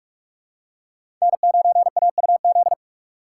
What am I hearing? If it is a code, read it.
Morse code, heard as a keyed tone, about 46 wpm, 700 Hz. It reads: N9AUZ
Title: USA Call Signs - 2x Character Spacing - 45wpm